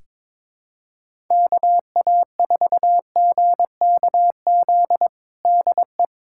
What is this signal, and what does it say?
Morse code, heard as a keyed tone, about 22 wpm, 700 Hz. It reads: KA4GKZ DE
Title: USA Callsign DE Callsign - 6 Characters - 22wpm